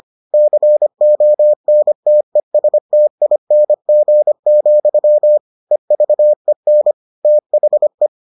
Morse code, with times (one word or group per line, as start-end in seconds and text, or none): 0.3-5.4 CONTESTING,
5.7-6.9 EVEN
7.2-8.1 THE